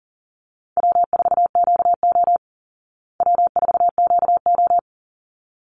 Morse code, heard as a keyed tone, about 40 wpm, 700 Hz. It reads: W4QO W4QO
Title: Call Signs Repeated 3x - 40wpm